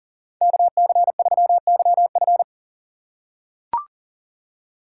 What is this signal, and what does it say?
Morse code, heard as a keyed tone, about 40 wpm, 700 Hz. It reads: KC3YF
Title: USA Call Signs - 5 Characters - 8x Character Spacing - 40wpm